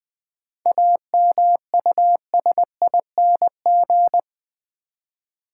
Morse code, heard as a keyed tone, about 20 wpm, 690 Hz. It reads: AMUSING
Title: 7 Letter Words - First 3 Letters Given - 20wpm